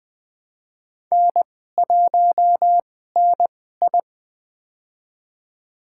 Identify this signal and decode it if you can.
Morse code, heard as a keyed tone, about 20 wpm, spaced as for 15 wpm, 700 Hz. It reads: N1NI
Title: USA Call Signs - 4 Characters - 2x Character Spacing - 20wpm